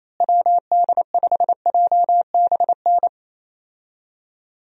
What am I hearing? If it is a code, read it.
Morse code, heard as a keyed tone, about 27 wpm, 700 Hz. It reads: WD5JBN
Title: USA Call Signs - 6 Characters - Repeated 2x - 28wpm